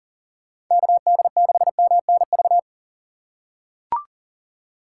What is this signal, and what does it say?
Morse code, heard as a keyed tone, about 40 wpm, 700 Hz. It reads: KD6MNV
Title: USA Call Signs - 6 Characters - 4x Character Spacing - 40wpm